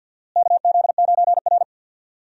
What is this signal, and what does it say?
Morse code, heard as a keyed tone, about 49 wpm, 700 Hz. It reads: KZ9R
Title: USA Call Signs - 4 Characters - Repeated 3x - 50wpm